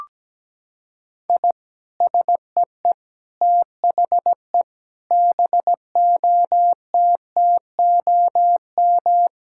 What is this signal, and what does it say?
Morse code, heard as a keyed tone, about 17 wpm, 700 Hz. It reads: I SEE THE BOTTOM
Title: Sentences from Top 1000 Words - Review Part 6 - 17wpm